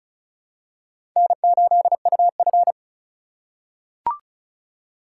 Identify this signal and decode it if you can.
Morse code, heard as a keyed tone, about 36 wpm, 700 Hz. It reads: N8UF
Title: USA Call Signs - 3x Character Spacing - 35wpm